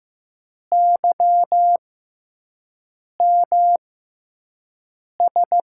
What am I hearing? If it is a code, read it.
Morse code, heard as a keyed tone, about 15 wpm, spaced as for 5 wpm, 700 Hz. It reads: YMS
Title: USA Call Signs - 5 Characters - 6x Character Spacing - 15wpm